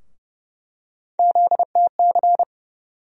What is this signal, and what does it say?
Morse code, heard as a keyed tone, about 32 wpm, 700 Hz. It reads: ZTC
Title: USA Call Sign Postfixes - 3 Letters 30wpm